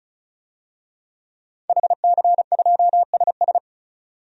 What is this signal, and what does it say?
Morse code, heard as a keyed tone, about 35 wpm, 700 Hz. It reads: HC2SS
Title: DXCC-100 Call Signs - 5 Characters - 35wpm